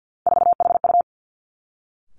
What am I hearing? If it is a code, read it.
Morse code, heard as a keyed tone, about 50 wpm, 700 Hz. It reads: 4HU
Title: Three Challenging Letter-Number Combinations 50wpm